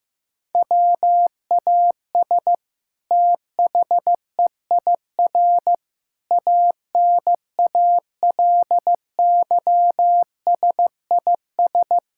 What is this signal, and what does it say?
Morse code, heard as a keyed tone, about 15 wpm, 700 Hz. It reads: WAS THEIR ANALYSIS
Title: Sentences from Top 1000 Words - Review Part 5 - 15wpm